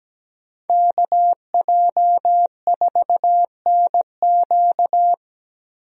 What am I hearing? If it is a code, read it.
Morse code, heard as a keyed tone, about 17 wpm, 700 Hz. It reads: KJ4NQ
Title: USA Call Signs - 5 Characters - Repeated 3x - 17wpm